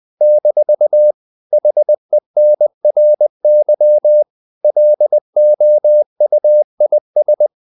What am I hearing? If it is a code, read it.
Morse code, heard as a keyed tone, about 20 wpm, 600 Hz. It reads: - HENRY LOUIS
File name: CW / Practice / EnglishQuotations0320